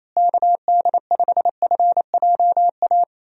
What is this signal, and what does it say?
Morse code, heard as a keyed tone, about 27 wpm, 700 Hz. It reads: KD5FJA
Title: USA Call Signs - 6 Characters - Repeated 2x - 28wpm